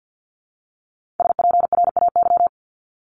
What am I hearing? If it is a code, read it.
Morse code, heard as a keyed tone, about 50 wpm, 700 Hz. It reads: SPRAY
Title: Multi-letter Phonemes Repeated 3x 50wpm